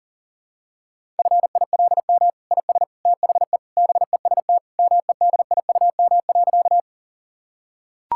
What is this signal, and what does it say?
Morse code, heard as a keyed tone, about 40 wpm, 700 Hz. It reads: FILM IS THE BEST MEDIUM.
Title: Sentences from Top 1000 Words - Review Part 6 - 40wpm